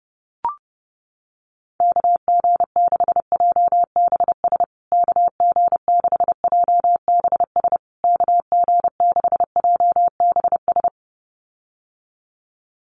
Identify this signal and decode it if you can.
Morse code, heard as a keyed tone, about 30 wpm, 700 Hz. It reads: KG6JBS KG6JBS KG6JBS
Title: USA Call Signs Repeated 3x 30wpm